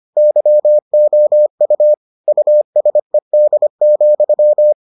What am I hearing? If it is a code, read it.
Morse code, heard as a keyed tone, about 25 wpm, 600 Hz. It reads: YOU USED,